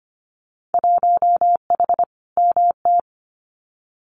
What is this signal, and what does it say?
Morse code, heard as a keyed tone, about 25 wpm, 700 Hz. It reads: 1H MT
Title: Winter Field Day Exchanges Encore 25wpm